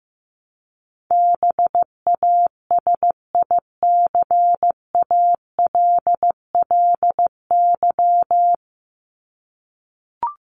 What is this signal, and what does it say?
Morse code, heard as a keyed tone, about 15 wpm, 700 Hz. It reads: BASICALLY
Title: Top 1251-1500 Words 15wpm